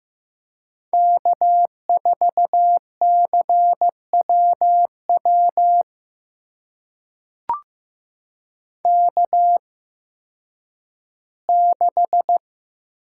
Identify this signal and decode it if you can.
Morse code, heard as a keyed tone, about 15 wpm, spaced as for 10 wpm, 700 Hz. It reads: K4CWW K6
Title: USA Call Signs - 5 Characters - 8x Character Spacing - 15wpm